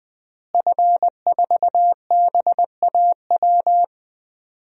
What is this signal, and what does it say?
Morse code, heard as a keyed tone, about 20 wpm, 700 Hz. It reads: F4BAW
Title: DXCC-100 Call Signs - 5 Characters - Repeated 3x - 20wpm